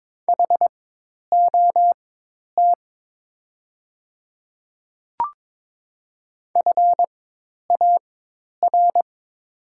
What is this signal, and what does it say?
Morse code, heard as a keyed tone, about 22 wpm, spaced as for 10 wpm, 700 Hz. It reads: HOT FAR
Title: 4x Character Spacing - Top 300 Words - 22wpm